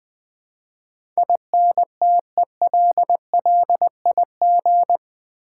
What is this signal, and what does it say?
Morse code, heard as a keyed tone, about 20 wpm, 700 Hz. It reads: INTELLIG
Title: Build A Word - 4 Letters At A Time - 12-Letter Words - 20wpm